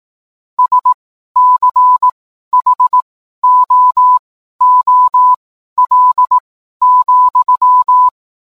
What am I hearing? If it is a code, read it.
Morse code, heard as a keyed tone, about 18 wpm, spaced as for 12 wpm, 1000 Hz. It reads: SCHOOL,
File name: cwblog / cw mp3 directory / mem morse 12